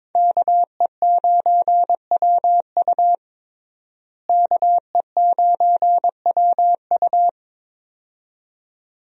Morse code, heard as a keyed tone, about 22 wpm, 700 Hz. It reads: KE9WU KE9WU
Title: USA Call Signs - 5 Characters - Repeated 2x - 22wpm